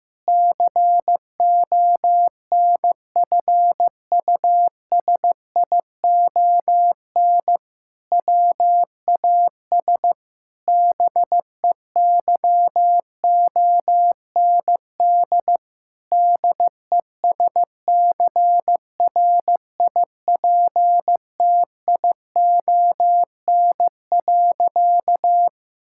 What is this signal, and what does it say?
Morse code, heard as a keyed tone, about 15 wpm, 700 Hz. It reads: CONFUSION WAS BEYOND DESCRIPTION.
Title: Sentences from Top 1600 Words 15wpm